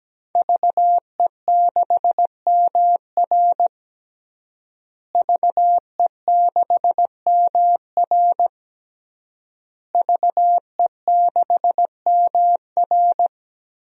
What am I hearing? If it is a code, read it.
Morse code, heard as a keyed tone, about 17 wpm, 700 Hz. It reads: VE6MR VE6MR VE6MR
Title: DXCC-100 Call Signs - 5 Characters - Repeated 3x - 17wpm